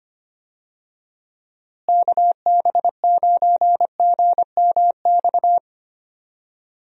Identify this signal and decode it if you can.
Morse code, heard as a keyed tone, about 25 wpm, 700 Hz. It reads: KB9GMX
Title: USA Call Signs - 6 Characters - Repeated 3x - 25wpm